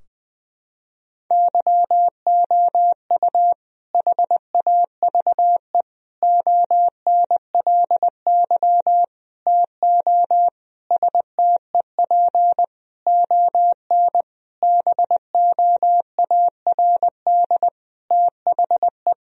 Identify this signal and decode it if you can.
Morse code, heard as a keyed tone, about 20 wpm, 700 Hz. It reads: YOU HAVE ONLY TO STEP ON BOARD THE
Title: Sentences from Top 1000 Words - Review Part 4 - 20wpm